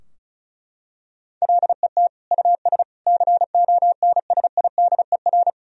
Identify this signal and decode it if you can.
Morse code, heard as a keyed tone, about 36 wpm, 700 Hz. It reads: LET US CONSIDER
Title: Sentences from Top 1000 Words - Review Part 4 - 35wpm